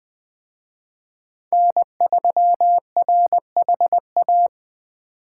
Morse code, heard as a keyed tone, about 21 wpm, 700 Hz. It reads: N3RHA
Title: Speed Racing - USA 1x3 Call Signs 17wpm